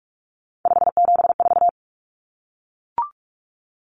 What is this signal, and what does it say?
Morse code, heard as a keyed tone, about 44 wpm, 700 Hz. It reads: 574
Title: Common Cut Numbers - 3-Digit 45wpm